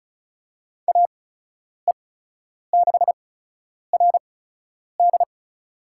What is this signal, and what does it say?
Morse code, heard as a keyed tone, about 36 wpm, spaced as for 10 wpm, 700 Hz. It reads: AE6RD
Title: USA Call Signs - 8x Character Spacing - 35wpm